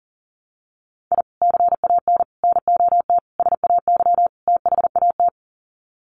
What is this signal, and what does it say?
Morse code, heard as a keyed tone, about 40 wpm, 700 Hz. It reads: I CAN NOT SAY THAT
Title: Sets of 5 Words Encore2 40wpm